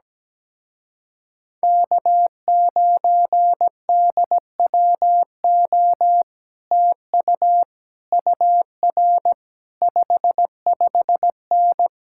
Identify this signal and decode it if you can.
Morse code, heard as a keyed tone, about 17 wpm, 700 Hz. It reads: K9DWO TU UR 55N
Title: Speed-Racing Common QSO Phrases - 17wpm